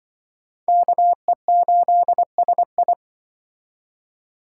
Morse code, heard as a keyed tone, about 25 wpm, 700 Hz. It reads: KE8SI
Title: Speed Racing - USA 2x2 Call Signs 20wpm